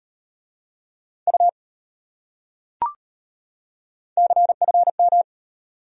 Morse code, heard as a keyed tone, about 33 wpm, 700 Hz. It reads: U CFM